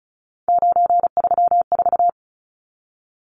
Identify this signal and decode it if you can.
Morse code, heard as a keyed tone, about 34 wpm, 700 Hz. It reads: 934
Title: Numbers - 3-Digit 35wpm